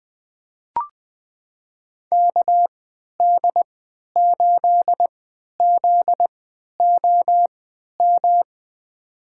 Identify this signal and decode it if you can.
Morse code, heard as a keyed tone, about 20 wpm, spaced as for 11 wpm, 700 Hz. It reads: KD8ZOM
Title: USA Call Signs - 3x Character Spacing - 20wpm